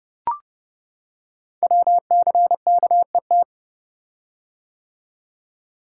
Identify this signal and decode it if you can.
Morse code, heard as a keyed tone, about 30 wpm, 700 Hz. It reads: WCKET
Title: Code Groups - 5 Letters - 30wpm